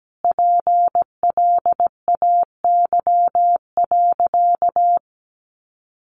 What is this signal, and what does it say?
Morse code, heard as a keyed tone, about 17 wpm, 700 Hz. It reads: PLAY.
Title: Sentences from Top 900 Words 17wpm